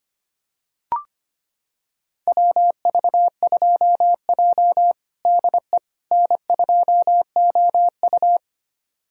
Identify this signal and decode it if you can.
Morse code, heard as a keyed tone, about 25 wpm, 700 Hz. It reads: WV2J DE N2OU
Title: USA Callsign DE Callsign - 4 Characters - 25wpm